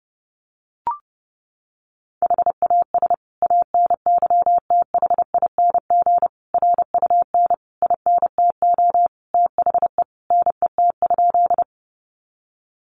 Morse code, heard as a keyed tone, about 30 wpm, 700 Hz. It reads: HAS ANYTHING RUN INTO THE NET?
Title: Sentences from Top 1400 Words 30wpm